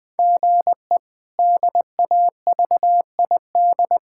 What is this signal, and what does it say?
Morse code, heard as a keyed tone, about 20 wpm, 700 Hz. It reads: GE DAVID
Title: K1USN SST QSO Practice - 20wpm